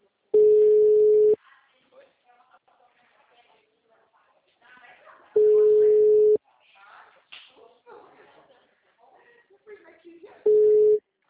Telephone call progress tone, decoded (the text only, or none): ringback tone